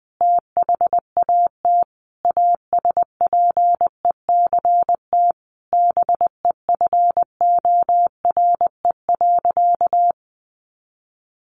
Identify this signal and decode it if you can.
Morse code, heard as a keyed tone, about 20 wpm, 700 Hz. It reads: THAT ASPECT BEFORE.
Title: Sentences from Top 1300 Words 20wpm